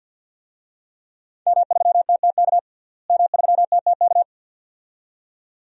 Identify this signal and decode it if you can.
Morse code, heard as a keyed tone, about 52 wpm, 700 Hz. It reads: M3TTK M3TTK
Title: DXCC-100 Call Signs - 5 Characters - Repeated 2x - 50wpm